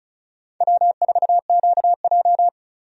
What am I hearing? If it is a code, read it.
Morse code, heard as a keyed tone, about 36 wpm, 700 Hz. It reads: W4QJ
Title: USA Call Signs - 4 Characters - 35wpm